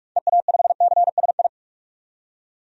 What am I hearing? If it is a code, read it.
Morse code, heard as a keyed tone, about 43 wpm, 700 Hz. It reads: EA5CSI
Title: DXCC-100 Call Signs - 6 Characters - 45wpm